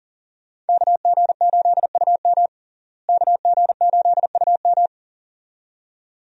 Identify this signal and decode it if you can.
Morse code, heard as a keyed tone, about 39 wpm, 700 Hz. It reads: KG8UM KG8UM
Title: USA Call Signs - 5 Characters - Repeated 2x - 40wpm